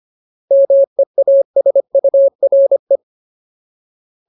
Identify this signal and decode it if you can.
Morse code, heard as a keyed tone, about 25 wpm, 550 Hz. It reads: MEASURE
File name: Ultimate Word List / 25WPM Ultimate